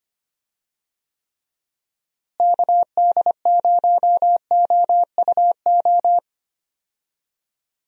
Morse code, heard as a keyed tone, about 25 wpm, 700 Hz. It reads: KD0OUO
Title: USA Call Signs - 6 Characters - Repeated 3x - 25wpm